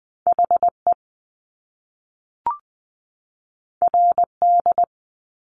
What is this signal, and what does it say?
Morse code, heard as a keyed tone, about 20 wpm, 700 Hz. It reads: HE RD
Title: Top Two Letter Combinations in Words 20wpm